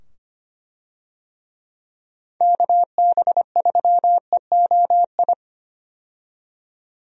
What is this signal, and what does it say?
Morse code, heard as a keyed tone, about 25 wpm, 700 Hz. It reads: KB3EOI
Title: USA Call Signs - 6 Characters - 25wpm